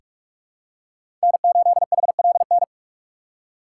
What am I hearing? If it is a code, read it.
Morse code, heard as a keyed tone, about 46 wpm, 700 Hz. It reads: N8HLN
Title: USA Call Signs - 4x Character Spacing - 45wpm